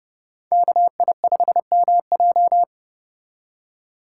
Morse code, heard as a keyed tone, about 38 wpm, 700 Hz. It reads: KI5MJ